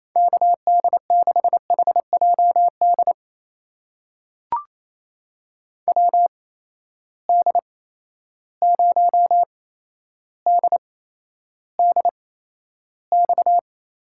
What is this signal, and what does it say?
Morse code, heard as a keyed tone, about 27 wpm, spaced as for 12 wpm, 700 Hz. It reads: KD6HJD WD0DDX
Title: USA Call Signs - 6 Characters - 8x Character Spacing - 28wpm